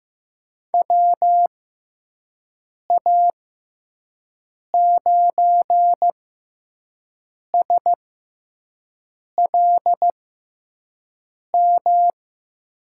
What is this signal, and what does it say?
Morse code, heard as a keyed tone, about 15 wpm, spaced as for 5 wpm, 700 Hz. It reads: WA9SLM
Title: USA Call Signs - 6 Characters - 6x Character Spacing - 15wpm